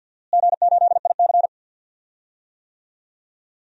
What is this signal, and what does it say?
Morse code, heard as a keyed tone, about 52 wpm, 700 Hz. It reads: G8IC
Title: DXCC-100 Call Signs - 4 Characters - Repeated 2x - 50wpm